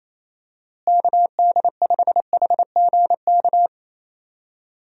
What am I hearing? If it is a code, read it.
Morse code, heard as a keyed tone, about 28 wpm, 700 Hz. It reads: KD5HGK
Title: USA Call Signs - 6 Characters - Repeated 3x - 28wpm